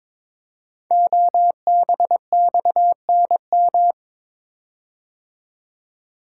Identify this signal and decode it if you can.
Morse code, heard as a keyed tone, about 22 wpm, 700 Hz. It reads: OBXNM
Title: Code Groups - 5 Letters - 22wpm